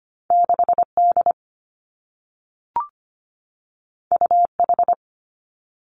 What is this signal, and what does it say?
Morse code, heard as a keyed tone, about 25 wpm, 700 Hz. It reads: 6 D U H
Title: Two Challenging Letter-Number Combinations 25wpm